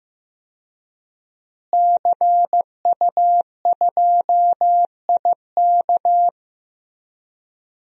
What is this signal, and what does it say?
Morse code, heard as a keyed tone, about 16 wpm, 700 Hz. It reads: CU2IK
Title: DXCC-100 Call Signs - 5 Characters - Repeated 2x - 15wpm